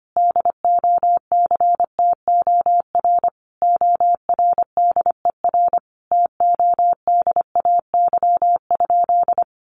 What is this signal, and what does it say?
Morse code, heard as a keyed tone, about 25 wpm, 700 Hz. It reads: DOCTOR ORDER TODAY?
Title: Sentences from Top 1500 Words 25wpm